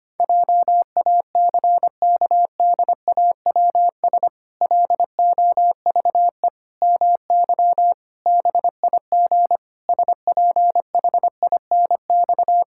Morse code, heard as a keyed tone, about 25 wpm, 700 Hz. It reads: JACKDAWS LOVE MY BIG SPHINX
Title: Pangram Challenge v2 - 25wpm